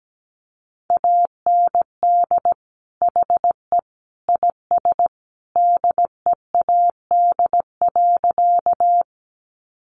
Morse code, heard as a keyed tone, about 17 wpm, 700 Hz. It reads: AND HE IS DEAD.
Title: Sentences from Top 1500 Words 17wpm